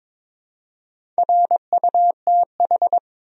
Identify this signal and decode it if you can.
Morse code, heard as a keyed tone, about 22 wpm, 700 Hz. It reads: RUTH
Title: USA Common Names - Max 4 Letters - 22wpm